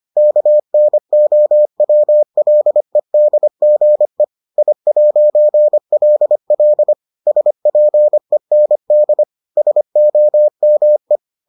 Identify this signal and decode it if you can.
Morse code, heard as a keyed tone, about 25 wpm, 600 Hz. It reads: KNOWLEDGE I'LL SPEND SOME